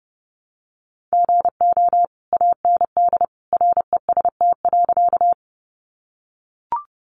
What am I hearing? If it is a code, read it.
Morse code, heard as a keyed tone, about 30 wpm, 700 Hz. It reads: GO AND REST.
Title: Sentences from Top 500 Words 30wpm